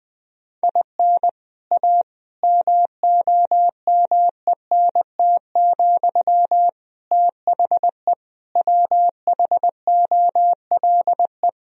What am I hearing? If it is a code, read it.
Morse code, heard as a keyed tone, about 20 wpm, 700 Hz. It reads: IN A MOMENT, THE WHOLE
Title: Sentences from Top 1000 Words - Review Part 3 - 20wpm